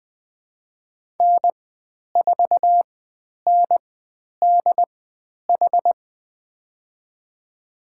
Morse code, heard as a keyed tone, about 20 wpm, spaced as for 10 wpm, 700 Hz. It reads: N4NDH
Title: 095 Lesson 16g - Letter c - Call Signs 20wpm